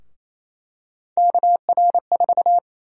Morse code, heard as a keyed tone, about 28 wpm, 700 Hz. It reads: KR4
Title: USA Call Sign Prefixes - 28wpm